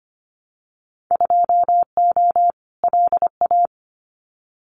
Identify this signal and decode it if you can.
Morse code, heard as a keyed tone, about 25 wpm, 700 Hz. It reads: 2O LA
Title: Winter Field Day Exchanges 25wpm